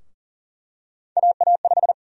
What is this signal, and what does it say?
Morse code, heard as a keyed tone, about 40 wpm, 700 Hz. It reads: AA5
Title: USA Call Sign Prefixes - 40wpm